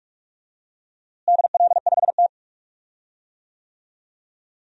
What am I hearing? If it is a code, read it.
Morse code, heard as a keyed tone, about 45 wpm, 700 Hz. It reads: DL5T
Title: DXCC-100 Call Signs - 4 Characters - Repeated 3x - 45wpm